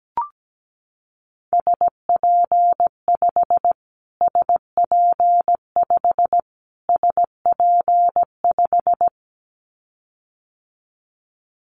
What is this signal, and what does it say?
Morse code, heard as a keyed tone, about 17 wpm, 700 Hz. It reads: SP5 SP5 SP5
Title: Callsign Prefixes for Top 25 DXCC - Repeated 3x - 17wpm